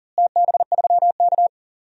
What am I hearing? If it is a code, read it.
Morse code, heard as a keyed tone, about 40 wpm, 700 Hz. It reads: TB3K
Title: DXCC-100 Call Signs - 4 Characters - Repeated 3x - 40wpm